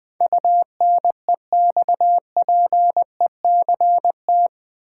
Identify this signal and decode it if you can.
Morse code, heard as a keyed tone, about 20 wpm, 700 Hz. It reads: UNEXPECT
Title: Build A Word - 4 Letters At A Time - 10-Letter Words - 20wpm